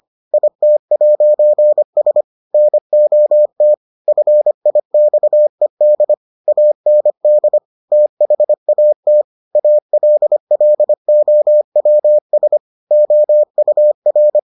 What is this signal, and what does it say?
Morse code, heard as a keyed tone, about 25 wpm, 600 Hz. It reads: IT'S NOT FIXED AND THAT ALLOWS OUR